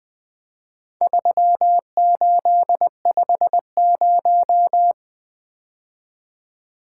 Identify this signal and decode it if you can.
Morse code, heard as a keyed tone, about 20 wpm, 700 Hz. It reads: 3850
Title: Four Letter-Number Combinations - 20wpm